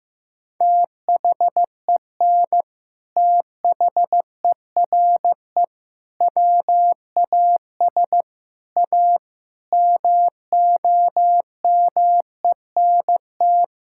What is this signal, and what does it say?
Morse code, heard as a keyed tone, about 15 wpm, 700 Hz. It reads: THEN THERE WAS A MOMENT
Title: Sentences from Top 1600 Words 15wpm